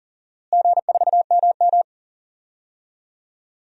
Morse code, heard as a keyed tone, about 40 wpm, 700 Hz. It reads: G4MM
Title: DXCC-100 Call Signs - 4 Characters - Repeated 2x - 40wpm